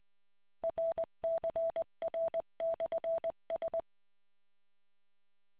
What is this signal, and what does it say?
Morse code, heard as a keyed tone, about 20 wpm, 660 Hz. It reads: RCR/S